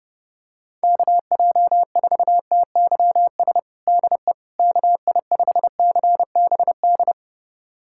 Morse code, heard as a keyed tone, about 32 wpm, 700 Hz. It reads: KJ4TYS DE KI5CBD
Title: USA Callsign DE Callsign - 6 Characters - 30wpm